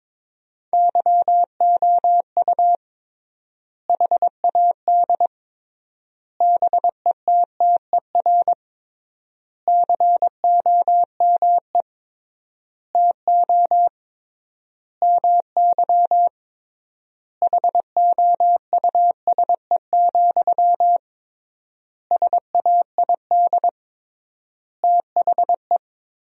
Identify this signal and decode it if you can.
Morse code, heard as a keyed tone, about 21 wpm, 700 Hz. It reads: YOU HAD BETTER COME TO MY HOUSE, SAID THE
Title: 3x Word-Spacing - Sentences from Top 400 Words - 22wpm